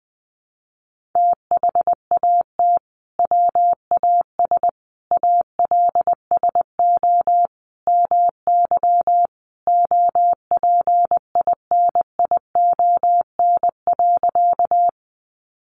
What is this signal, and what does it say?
Morse code, heard as a keyed tone, about 20 wpm, 700 Hz. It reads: THAT WAS ALSO MY OPINION.
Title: Sentences from Top 1200 Words 20wpm